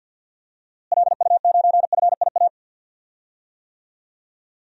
Morse code, heard as a keyed tone, about 52 wpm, 700 Hz. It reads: LU9FIA
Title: DXCC-100 Call Signs - 6 Characters - Repeated 3x - 50wpm